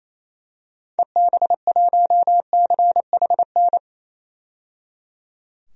Morse code, heard as a keyed tone, about 28 wpm, 700 Hz. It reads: EB1CHN
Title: DXCC-100 Call Signs - 6 Characters - 28wpm